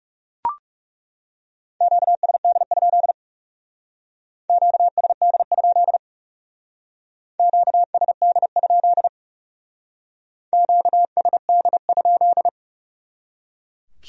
Morse code, heard as a keyed tone, about 38 wpm, 700 Hz. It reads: QSD? QSD? QSD? QSD?